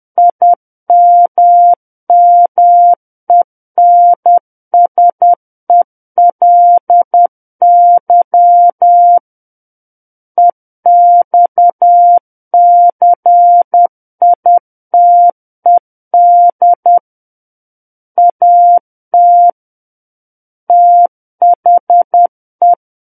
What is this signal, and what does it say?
Morse code, heard as a keyed tone, about 10 wpm, 700 Hz. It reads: IMMENSELY EXCITED AT THE
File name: Book One / 01-Chapter 1 - 10 WPM